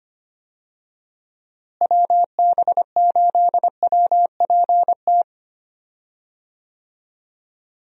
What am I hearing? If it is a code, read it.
Morse code, heard as a keyed tone, about 25 wpm, 700 Hz. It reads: WB8WPT
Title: USA Call Signs - 6 Characters - Repeated 3x - 25wpm